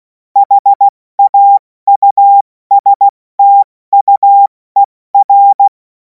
Morse code, heard as a keyed tone, about 16 wpm, 800 Hz. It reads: HAUSTUER
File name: Morsekurs / Witze / Witz 24. März 2026 70 F80